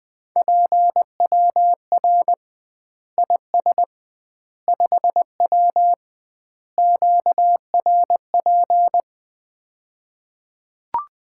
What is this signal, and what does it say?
Morse code, heard as a keyed tone, about 21 wpm, 700 Hz. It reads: PWR IS 5W QRP
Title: Common QSO Phrases - 2x Word-Spacing - 20wpm